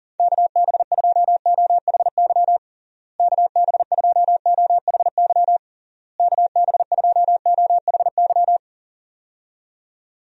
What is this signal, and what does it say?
Morse code, heard as a keyed tone, about 40 wpm, 700 Hz. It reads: KB2OHY KB2OHY KB2OHY
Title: USA Call Signs - 6 Characters - Repeated 3x - 40wpm